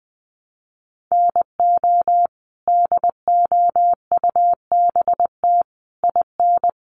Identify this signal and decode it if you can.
Morse code, heard as a keyed tone, about 20 wpm, 700 Hz. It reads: NO DOUBT IN
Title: Sets of 5 Words Encore2 20wpm